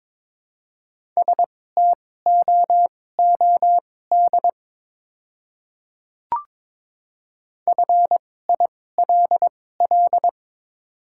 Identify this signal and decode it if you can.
Morse code, heard as a keyed tone, about 22 wpm, spaced as for 16 wpm, 690 Hz. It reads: STOOD FILL
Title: 2x Character Spacing - Top 500 Words - 22wpm